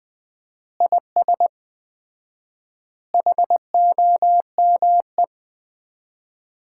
Morse code, heard as a keyed tone, about 20 wpm, 700 Hz. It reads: IS HOME
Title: Common QSO Phrases - 4x Word-Spacing - 20wpm